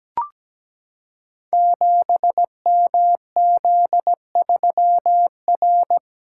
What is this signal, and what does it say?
Morse code, heard as a keyed tone, about 17 wpm, 700 Hz. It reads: 7MZ3R
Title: Code Groups - 5 Letter:Numbers - 17wpm